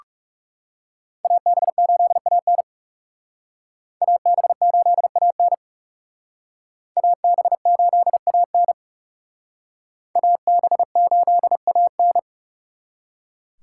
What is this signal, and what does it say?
Morse code, heard as a keyed tone, about 39 wpm, 700 Hz. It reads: AB8AN AB8AN AB8AN AB8AN